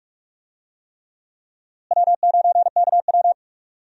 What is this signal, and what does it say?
Morse code, heard as a keyed tone, about 44 wpm, 700 Hz. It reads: W9KW
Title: USA Call Signs - 4 Characters - Repeated 3x - 45wpm